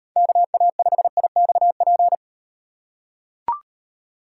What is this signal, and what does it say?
Morse code, heard as a keyed tone, about 38 wpm, 700 Hz. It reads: KA5IXP